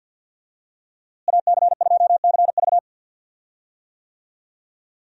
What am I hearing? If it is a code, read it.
Morse code, heard as a keyed tone, about 52 wpm, 700 Hz. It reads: AC2CV
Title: USA Call Signs - 5 Characters - Repeated 3x - 50wpm